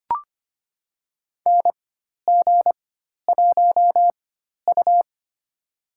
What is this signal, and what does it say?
Morse code, heard as a keyed tone, about 25 wpm, spaced as for 12 wpm, 700 Hz. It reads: NG1U
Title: USA Call Signs - 4 Characters - 4x Character Spacing - 25wpm